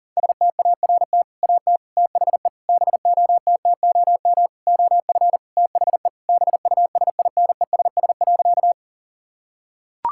Morse code, heard as a keyed tone, about 40 wpm, 700 Hz. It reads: START AT THE BOTTOM OF THE BUSINESS.
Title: Sentences from Top 1000 Words - Review Part 6 - 40wpm